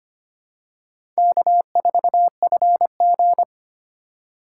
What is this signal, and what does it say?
Morse code, heard as a keyed tone, about 25 wpm, 700 Hz. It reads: K4FG
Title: USA Call Signs - 4 Characters - 3x Character Spacing - 25wpm